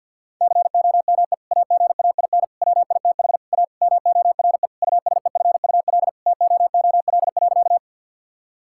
Morse code, heard as a keyed tone, about 50 wpm, 700 Hz. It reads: COME AGAIN WITH A MORE USEFUL TOOL.
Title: Sentences from Top 1000 Words - Review Part 6 - 50wpm